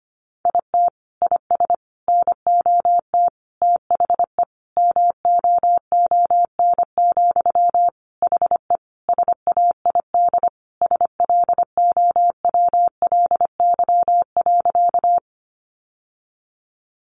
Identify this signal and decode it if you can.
Morse code, heard as a keyed tone, about 25 wpm, 700 Hz. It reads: IT IS NOT THE MOON, HE SAID SLOWLY.
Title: Sentences from Top 1100 Words 25wpm